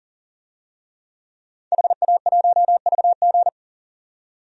Morse code, heard as a keyed tone, about 40 wpm, 700 Hz. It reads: HA1VG
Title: DXCC-100 Call Signs - 5 Characters - 40wpm